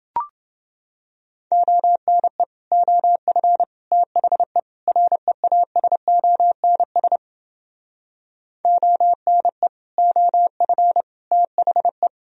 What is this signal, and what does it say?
Morse code, heard as a keyed tone, about 29 wpm, 700 Hz. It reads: ONE OF THE REASONS ONE OF THE